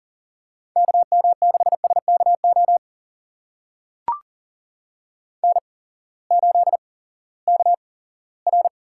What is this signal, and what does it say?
Morse code, heard as a keyed tone, about 39 wpm, spaced as for 20 wpm, 700 Hz. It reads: KM6SKO N8KR
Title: USA Call Signs - 8x Character Spacing - 40wpm